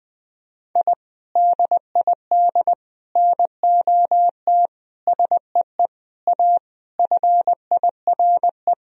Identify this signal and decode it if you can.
Morse code, heard as a keyed tone, about 19 wpm, 700 Hz. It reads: I DID NOT SEE A FIRE
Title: Sentences from Top 1000 Words - Review Part 2 - 20wpm